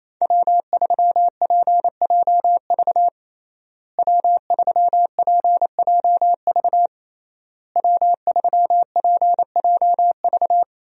Morse code, heard as a keyed tone, about 28 wpm, 700 Hz. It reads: W3PJV W3PJV W3PJV
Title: USA Call Signs - 5 Characters - Repeated 3x - 28wpm